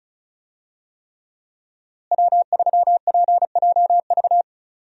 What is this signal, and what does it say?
Morse code, heard as a keyed tone, about 34 wpm, 700 Hz. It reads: W3PJV
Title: USA Call Signs - 5 Characters - Repeated 3x - 35wpm